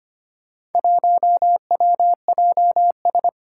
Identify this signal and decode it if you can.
Morse code, heard as a keyed tone, about 25 wpm, 700 Hz. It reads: 1WJS
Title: Four Challenging Letter-Number Combinations 25wpm